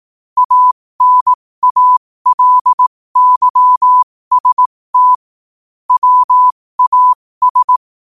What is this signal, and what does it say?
Morse code, heard as a keyed tone, about 18 wpm, spaced as for 15 wpm, 1000 Hz. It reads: ANALYST WAS
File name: qotdfiles / 15wpm